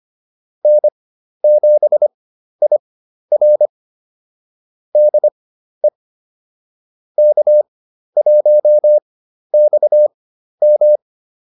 Morse code, heard as a keyed tone, about 25 wpm, spaced as for 12 wpm, 600 Hz. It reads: N7IR DE K1XM